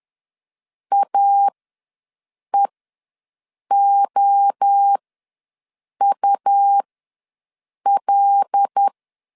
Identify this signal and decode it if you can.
Morse code, heard as a keyed tone, about 10 wpm, spaced as for 6 wpm, 800 Hz. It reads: AEOUL
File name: LEZIONI / 5 LBVC / 5 LB ws 10wpm 3616